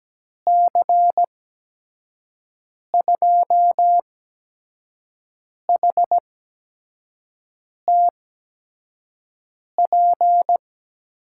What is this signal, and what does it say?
Morse code, heard as a keyed tone, about 17 wpm, spaced as for 5 wpm, 700 Hz. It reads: C2HTP
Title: USA Call Signs - 6 Characters - 8x Character Spacing - 17wpm